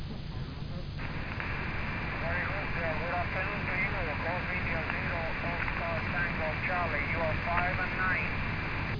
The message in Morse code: none